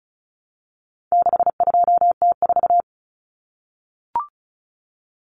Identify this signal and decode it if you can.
Morse code, heard as a keyed tone, about 34 wpm, 700 Hz. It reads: 62T4
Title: Common Cut Numbers - 4-Digit 35wpm